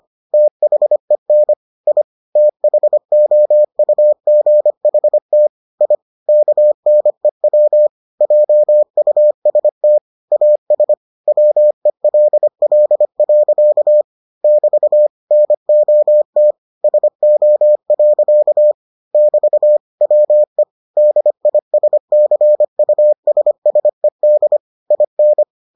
Morse code, heard as a keyed tone, about 25 wpm, 600 Hz. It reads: THEN I THOUGHT I KNEW JUST AS WELL. <BT> NOT SO. <BT> WE DISCUSSED IN